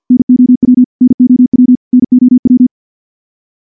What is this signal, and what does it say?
Morse code, heard as a keyed tone, about 50 wpm, 250 Hz. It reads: NOW NOW NOW
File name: morse / morse-training-50WPM